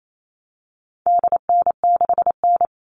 Morse code, heard as a keyed tone, about 28 wpm, 700 Hz. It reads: DN6N
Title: All Cut Numbers - 4-Digit 28wpm